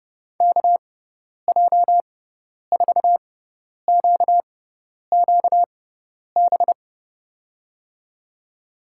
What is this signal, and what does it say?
Morse code, heard as a keyed tone, about 30 wpm, spaced as for 10 wpm, 700 Hz. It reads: KJ4QQB
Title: USA Call Signs - 6x Character Spacing - 30wpm